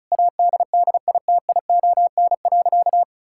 Morse code, heard as a keyed tone, about 36 wpm, 700 Hz. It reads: ADDITION.
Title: Sentences from Top 1000 Words - Review Part 6 - 35wpm